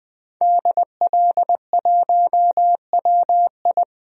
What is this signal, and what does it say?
Morse code, heard as a keyed tone, about 20 wpm, 700 Hz. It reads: DL1WI
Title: DXCC-100 Call Signs - 5 Characters - 20wpm